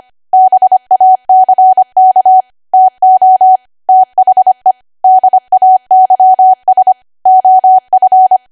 Morse code, heard as a keyed tone, about 25 wpm, 750 Hz. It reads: BACK TO THE DAYS OF
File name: Archive / 25 WPM / 25WPM